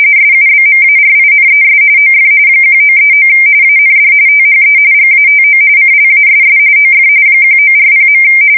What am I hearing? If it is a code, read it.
RTTY, about 45.45 baud. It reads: SUBMARINES OF 1939 AND THE GATO CLASS OF 1940.
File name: fsk170 / usnavy4